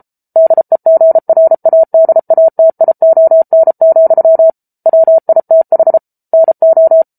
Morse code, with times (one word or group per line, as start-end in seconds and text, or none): 0.4-4.5 DEGRADATION,
4.9-6.0 WITH
6.3-7.0 NO